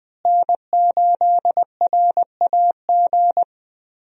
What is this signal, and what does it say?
Morse code, heard as a keyed tone, about 20 wpm, 700 Hz. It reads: N8RAG
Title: USA Call Signs - 5 Characters - Repeated 2x - 20wpm